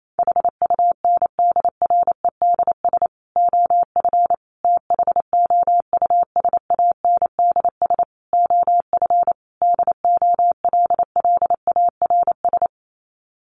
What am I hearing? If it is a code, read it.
Morse code, heard as a keyed tone, about 28 wpm, 700 Hz. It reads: HUNDREDS OF THOUSANDS OF DOLLARS
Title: Sets of 5 Words Encore 28wpm